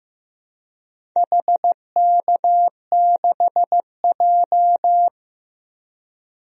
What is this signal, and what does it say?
Morse code, heard as a keyed tone, about 15 wpm, 700 Hz. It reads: HK6J
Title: DXCC-100 Call Signs - 4 Characters - 15wpm